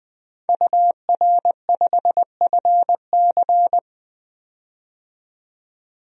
Morse code, heard as a keyed tone, about 20 wpm, 700 Hz. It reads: UR5FC
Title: DXCC-100 Call Signs - 5 Characters - 20wpm